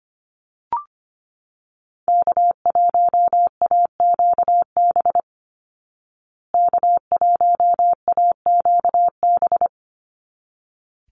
Morse code, heard as a keyed tone, about 25 wpm, 700 Hz. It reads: K1AQB K1AQB
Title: Call Signs Repeated 3x - 25wpm